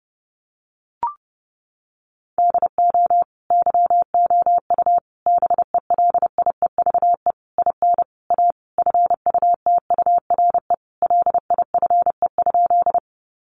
Morse code, heard as a keyed tone, about 30 wpm, 700 Hz. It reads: DO YOU BELIEVE IN A FUTURE LIFE?
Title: Sentences from Top 1000 Words 30wpm